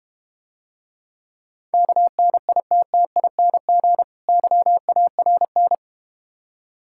Morse code, heard as a keyed tone, about 32 wpm, 700 Hz. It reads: KNITTING YARN